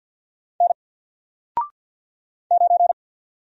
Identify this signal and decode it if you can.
Morse code, heard as a keyed tone, about 48 wpm, 700 Hz. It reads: N 9
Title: Single Character (Letter:Number:Punctuation:Prosign) 50wpm